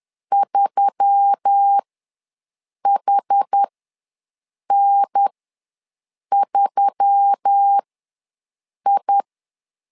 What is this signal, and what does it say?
Morse code, heard as a keyed tone, about 10 wpm, spaced as for 6 wpm, 800 Hz. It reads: 3HN3I
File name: LEZIONI / 4 Q1S3 ws 10wpm 2437